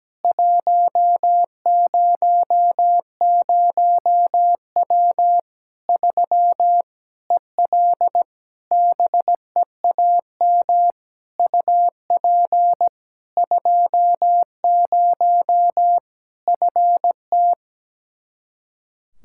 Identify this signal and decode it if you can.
Morse code, heard as a keyed tone, about 17 wpm, 700 Hz. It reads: 100W 3 EL BEAM UP 20 FT
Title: Practice QSO - Exchange 2 - Info - 17wpm